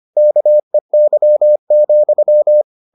Morse code, heard as a keyed tone, about 25 wpm, 600 Hz. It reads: KEY,